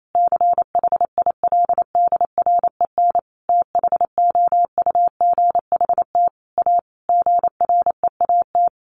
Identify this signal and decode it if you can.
Morse code, heard as a keyed tone, about 28 wpm, 700 Hz. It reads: CHILDREN THOUGHT A GREAT
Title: Sentences from Top 800 Words 28wpm